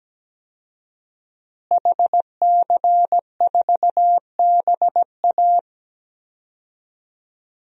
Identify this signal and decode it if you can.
Morse code, heard as a keyed tone, about 17 wpm, 700 Hz. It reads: HC4BA
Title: DXCC-100 Call Signs - 5 Characters - Repeated 3x - 17wpm